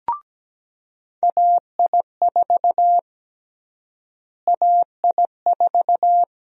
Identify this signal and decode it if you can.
Morse code, heard as a keyed tone, about 17 wpm, 700 Hz. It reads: AI4 AI4
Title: USA Call Sign Prefixes - Repeated 2x - 17wpm